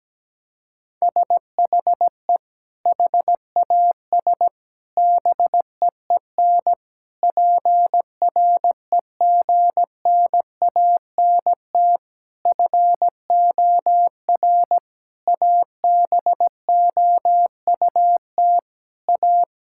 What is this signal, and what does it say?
Morse code, heard as a keyed tone, about 17 wpm, 700 Hz. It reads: SHE HAS BEEN PREGNANT FOR ABOUT A
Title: Sentences from Top 1000 Words - Review Part 6 - 17wpm